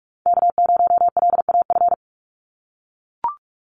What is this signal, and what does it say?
Morse code, heard as a keyed tone, about 44 wpm, 700 Hz. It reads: K0LAF
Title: NA SOTA Chaser Callsigns 45wpm